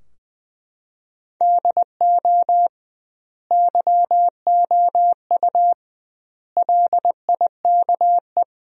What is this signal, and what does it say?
Morse code, heard as a keyed tone, about 21 wpm, 700 Hz. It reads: DO YOU LIKE
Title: 2x Word-Spacing - Sentences from Top 100 Words - 20wpm